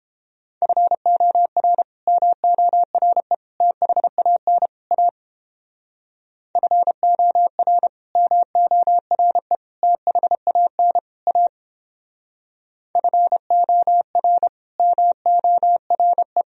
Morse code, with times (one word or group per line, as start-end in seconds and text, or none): 0.6-1.8 FOR
2.1-3.3 MORE
3.6-4.7 THAN
4.9-5.1 A
6.6-7.9 FOR
8.2-9.6 MORE
9.8-11.0 THAN
11.3-11.5 A
13.0-14.5 FOR
14.8-16.4 MORE